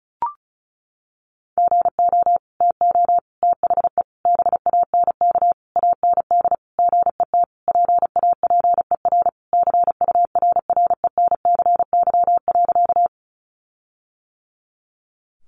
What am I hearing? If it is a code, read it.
Morse code, heard as a keyed tone, about 34 wpm, 700 Hz. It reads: GO TO THE BANK AND GET PAPER CURRENCY.
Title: Sentences from Top 1500 Words 35wpm